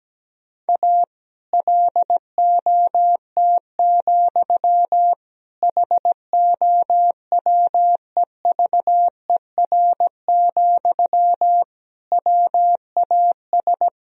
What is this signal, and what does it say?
Morse code, heard as a keyed tone, about 17 wpm, 700 Hz. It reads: A LOT, HOWEVER, WAS
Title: Sentences from Top 1000 Words - Review Part 3 - 17wpm